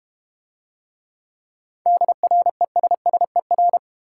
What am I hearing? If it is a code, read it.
Morse code, heard as a keyed tone, about 32 wpm, 700 Hz. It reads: DRESSER